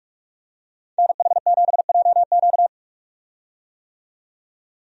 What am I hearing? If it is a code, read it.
Morse code, heard as a keyed tone, about 42 wpm, 700 Hz. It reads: NH7JQ
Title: USA Call Signs - 5 Characters - 45wpm